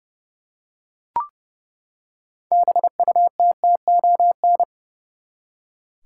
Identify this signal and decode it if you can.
Morse code, heard as a keyed tone, about 30 wpm, 700 Hz. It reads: BUTTON
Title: Six Letter Words - 30wpm